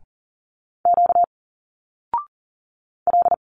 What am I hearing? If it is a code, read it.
Morse code, heard as a keyed tone, about 40 wpm, 700 Hz. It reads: Q L
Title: Single Letters 40wpm